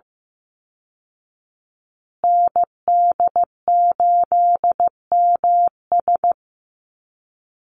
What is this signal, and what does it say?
Morse code, heard as a keyed tone, about 15 wpm, 700 Hz. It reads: ND8MS
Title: Call Signs Repeated 2x - 15wpm